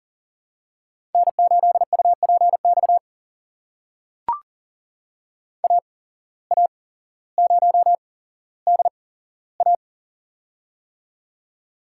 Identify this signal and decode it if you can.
Morse code, heard as a keyed tone, about 38 wpm, spaced as for 17 wpm, 700 Hz. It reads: N8UPX AA0DA
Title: USA Call Signs - 5 Characters - 8x Character Spacing - 40wpm